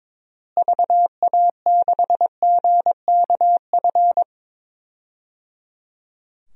Morse code, heard as a keyed tone, about 22 wpm, 700 Hz. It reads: VA6GKF
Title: DXCC-100 Call Signs - 6 Characters - Repeated 3x - 22wpm